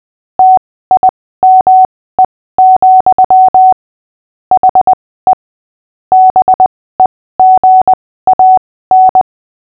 Morse code, heard as a keyed tone, about 20 wpm, spaced as for 15 wpm, 750 Hz. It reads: TIME, HE BEGAN
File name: cw / fists / Short-Story-6-15wpm